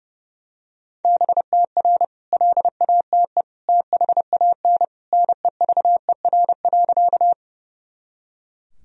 Morse code, heard as a keyed tone, about 30 wpm, 700 Hz. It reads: BTR LATE THAN NEVER.
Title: Sentences from Top 400 Words and Abbreviations 30wpm